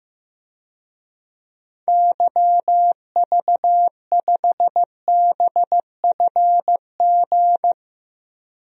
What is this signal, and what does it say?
Morse code, heard as a keyed tone, about 15 wpm, 700 Hz. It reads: YV5BFG
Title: DXCC-100 Call Signs - 6 Characters - 15wpm